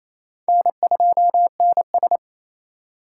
Morse code, heard as a keyed tone, about 28 wpm, 700 Hz. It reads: N2NS
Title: K1USN SST QSO Practice - 28wpm